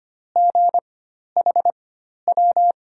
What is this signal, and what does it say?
Morse code, heard as a keyed tone, about 25 wpm, spaced as for 12 wpm, 700 Hz. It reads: GHW
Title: USA Call Signs - 6 Characters - 4x Character Spacing - 25wpm